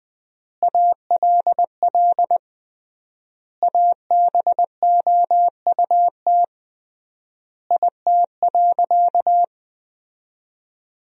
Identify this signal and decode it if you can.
Morse code, heard as a keyed tone, about 20 wpm, 700 Hz. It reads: ALL ABOUT IT.
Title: 3x Word-Spacing - Sentences from Top 400 Words - 20wpm